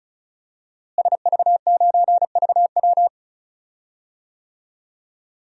Morse code, heard as a keyed tone, about 36 wpm, 700 Hz. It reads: SV9VW
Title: DXCC-100 Call Signs - 5 Characters - Repeated 3x - 35wpm